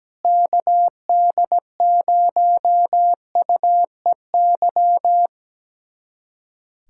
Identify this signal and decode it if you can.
Morse code, heard as a keyed tone, about 17 wpm, 700 Hz. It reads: KD0UEY
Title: USA Callsign DE Callsign 17wpm